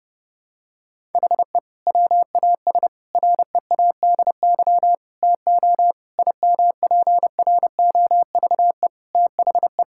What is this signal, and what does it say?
Morse code, heard as a keyed tone, about 30 wpm, 700 Hz. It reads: HE WAS READY TO IMPROVE THE
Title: Sentences from Top 1000 Words - Review Part 5 - 30wpm